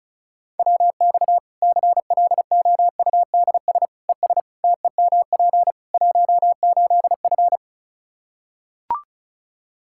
Morse code, heard as a keyed tone, about 34 wpm, 700 Hz. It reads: WX CLOUDS ES TEMP 18F
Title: Practice QSO - Exchange 2 - Info - 35wpm